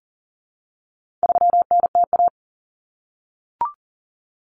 Morse code, heard as a keyed tone, about 40 wpm, 700 Hz. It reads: 3NTA
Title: Common Cut Numbers - 4-Digit 40wpm